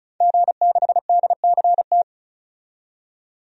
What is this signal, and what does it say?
Morse code, heard as a keyed tone, about 36 wpm, 700 Hz. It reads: G6DCT
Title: DXCC-100 Call Signs - 5 Characters - 35wpm